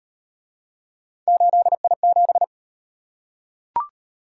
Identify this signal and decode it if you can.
Morse code, heard as a keyed tone, about 38 wpm, 700 Hz. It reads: 8I7